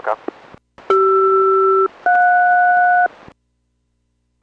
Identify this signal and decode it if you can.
Aircraft SELCAL, heard as two pairs of tones, with speech. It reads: CR-JS